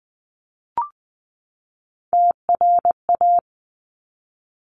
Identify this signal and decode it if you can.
Morse code, heard as a keyed tone, about 20 wpm, 700 Hz. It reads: TRA
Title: Top Three Letter Combinations in Words 20wpm